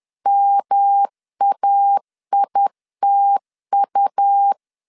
Morse code, heard as a keyed tone, about 10 wpm, 800 Hz. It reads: MAITU
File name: CW / DET / 1 3 1500C S B 10wpm 3335